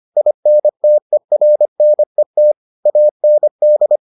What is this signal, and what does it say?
Morse code, heard as a keyed tone, about 25 wpm, 600 Hz. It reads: INTERNET AND